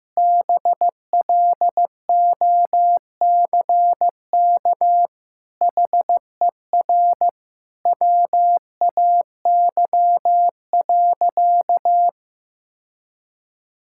Morse code, heard as a keyed tone, about 15 wpm, 700 Hz. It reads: BLOCK HER WAY.
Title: Sentences from Top 1600 Words 15wpm